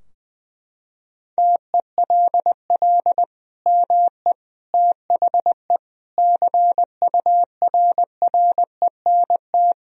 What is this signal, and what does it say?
Morse code, heard as a keyed tone, about 21 wpm, 700 Hz. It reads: TELL ME THE CURRENT
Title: Sentences from Top 1000 Words - Review Part 4 - 20wpm